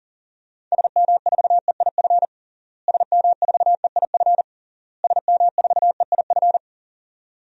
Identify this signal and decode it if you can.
Morse code, heard as a keyed tone, about 40 wpm, 700 Hz. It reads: SM4EIF SM4EIF SM4EIF
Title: DXCC-100 Call Signs - 6 Characters - Repeated 3x - 40wpm